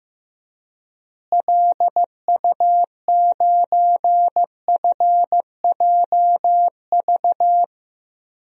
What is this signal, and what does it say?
Morse code, heard as a keyed tone, about 15 wpm, 700 Hz. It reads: LU9FJV
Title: DXCC-100 Call Signs - 6 Characters - Repeated 3x - 15wpm